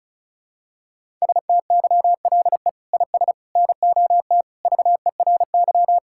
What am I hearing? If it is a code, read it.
Morse code, heard as a keyed tone, about 36 wpm, 700 Hz. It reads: STYLE IS NOT VERY
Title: Sentences from Top 1000 Words - Review Part 5 - 35wpm